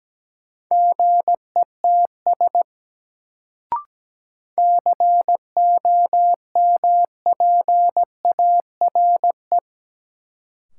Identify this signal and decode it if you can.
Morse code, heard as a keyed tone, about 17 wpm, 700 Hz. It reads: GETS COMPARE
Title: Sentences from Top 1300 Words - New Word Drill - 17wpm